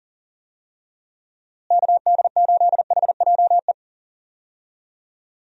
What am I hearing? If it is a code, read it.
Morse code, heard as a keyed tone, about 40 wpm, 700 Hz. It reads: KD8HJE
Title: USA Call Signs - 6 Characters - Repeated 2x - 40wpm